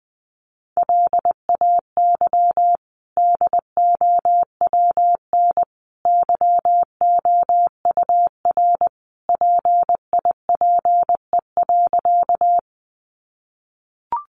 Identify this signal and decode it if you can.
Morse code, heard as a keyed tone, about 20 wpm, 700 Hz. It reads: LAY DOWN YOUR PIPE.
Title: Sentences from Top 1500 Words 20wpm